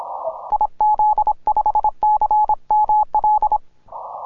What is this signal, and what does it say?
Morse code, heard as a keyed tone, about 26 wpm, 850 Hz. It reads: IZ5CML
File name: simple-file-list / tx5k 10mcw